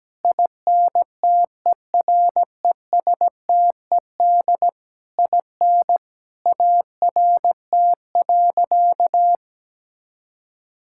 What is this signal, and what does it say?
Morse code, heard as a keyed tone, about 17 wpm, 700 Hz. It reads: INTERESTED IN ART.
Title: Sentences from Top 1000 Words - Review Part 6 - 17wpm